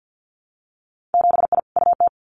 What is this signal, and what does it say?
Morse code, heard as a keyed tone, about 50 wpm, 700 Hz. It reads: 7I UT
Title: Winter Field Day Exchanges Encore 50wpm